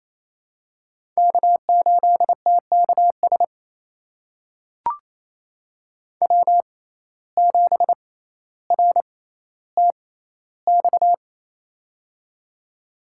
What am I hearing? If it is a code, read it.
Morse code, heard as a keyed tone, about 28 wpm, spaced as for 14 wpm, 700 Hz. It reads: K8TKS W7RTX
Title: USA Call Signs - 6x Character Spacing - 28wpm